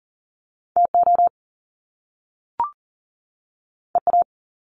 Morse code, heard as a keyed tone, about 42 wpm, 700 Hz. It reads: TO EA
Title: Top Two Letter Combinations in Words 40wpm